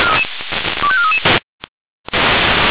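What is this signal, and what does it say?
Five-tone selective call (ZVEI), heard as quick runs of five tones, tones of about 70 ms. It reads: 25433
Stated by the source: Received on an SDR; ZVEI five-tone call.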